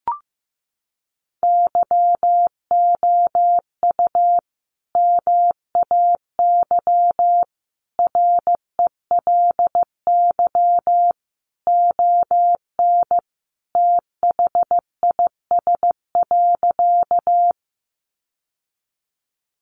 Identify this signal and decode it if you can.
Morse code, heard as a keyed tone, about 15 wpm, 700 Hz. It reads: YOU MAY RELY ON THIS.
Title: Sentences from Top 1200 Words 15wpm